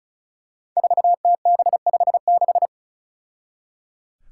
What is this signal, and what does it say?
Morse code, heard as a keyed tone, about 34 wpm, 700 Hz. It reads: 4TB56
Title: Five Challenging Letter-Number Combinations 35wpm